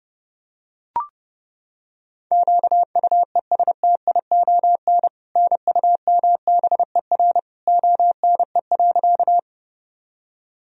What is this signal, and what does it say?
Morse code, heard as a keyed tone, about 29 wpm, 700 Hz. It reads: QUESTION NUMBER ONE.
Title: Sentences from Top 1000 Words - Review Part 1 - 30wpm